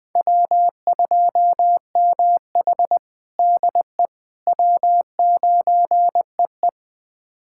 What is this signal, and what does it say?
Morse code, heard as a keyed tone, about 20 wpm, 700 Hz. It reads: W2MH DE W9EE
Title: USA Callsign DE Callsign - 4 Characters - 20wpm